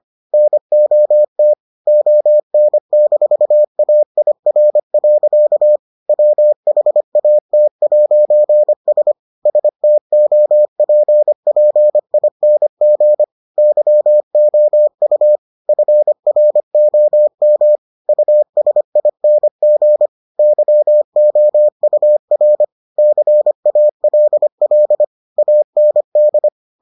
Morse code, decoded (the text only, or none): NOT ON-AIR. WHAT'S STOPPING YOU FROM USING YOUR CALL AND